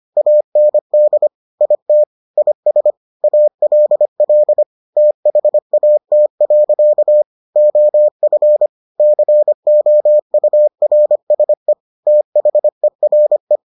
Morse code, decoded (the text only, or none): AND IT IS ALL THAT. OF COURSE THERE